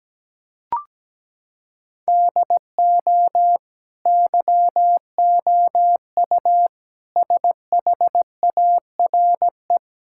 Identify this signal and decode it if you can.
Morse code, heard as a keyed tone, about 17 wpm, 700 Hz. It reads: DO YOU SHARE
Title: Sentences from Top 1600 Words 17wpm